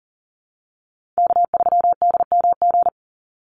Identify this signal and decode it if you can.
Morse code, heard as a keyed tone, about 39 wpm, 700 Hz. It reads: K3DMG
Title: Call Signs Repeated 2x - 40wpm